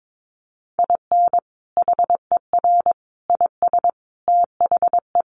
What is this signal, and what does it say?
Morse code, heard as a keyed tone, about 22 wpm, 700 Hz. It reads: IN HER IS THE
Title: Sentences from Top 1400 Words 22wpm